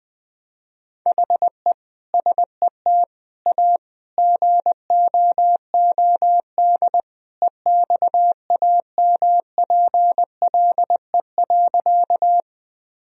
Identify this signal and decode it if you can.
Morse code, heard as a keyed tone, about 20 wpm, 700 Hz. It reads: HE SET A GOOD EXAMPLE.
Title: Sentences from Top 1000 Words - Review Part 1 - 20wpm